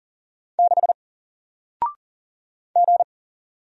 Morse code, heard as a keyed tone, about 40 wpm, 700 Hz. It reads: 6 G
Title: Single Character (Letter:Number:Punctuation:Prosign) 40wpm